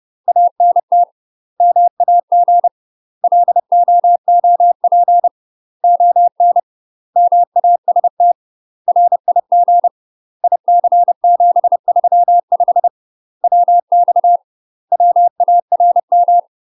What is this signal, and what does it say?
Morse code, heard as a keyed tone, about 30 wpm, 700 Hz. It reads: ANT MAG LOOP ON MAST RIG IC735 WX WARM
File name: UK MASTER QSOs TRAINING / 30WPM / UK MASTER QSOs 0017